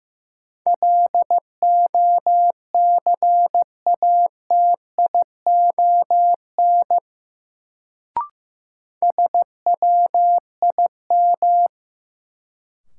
Morse code, heard as a keyed tone, about 15 wpm, 700 Hz. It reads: LOCATION SWIM
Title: Sentences from Top 1000 Words - Review Part 6 - New Word Drill - 15wpm